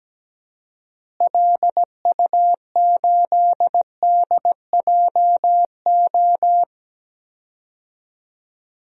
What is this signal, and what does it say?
Morse code, heard as a keyed tone, about 17 wpm, 700 Hz. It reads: LU8DJO
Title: DXCC-100 Call Signs - 6 Characters - Repeated 3x - 17wpm